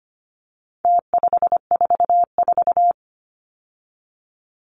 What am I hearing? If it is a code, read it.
Morse code, heard as a keyed tone, about 25 wpm, 700 Hz. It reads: T544
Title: Numbers - Contest Style - 4-Digit 25wpm